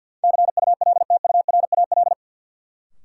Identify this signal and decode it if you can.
Morse code, heard as a keyed tone, about 50 wpm, 700 Hz. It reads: CULTURAL
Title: Sentences from Top 1500 Words - New Word Drill - 50wpm